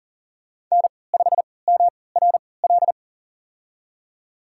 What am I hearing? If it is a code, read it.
Morse code, heard as a keyed tone, about 38 wpm, spaced as for 22 wpm, 700 Hz. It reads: N5MRL
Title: USA Call Signs - 5 Characters - 3x Character Spacing - 40wpm